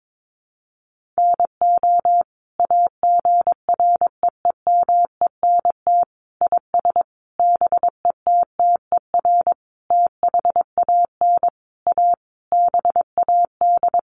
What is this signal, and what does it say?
Morse code, heard as a keyed tone, about 22 wpm, 700 Hz. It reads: NO AGREEMENT IS BETTER THAN A BAD
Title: Sentences from Top 1500 Words 22wpm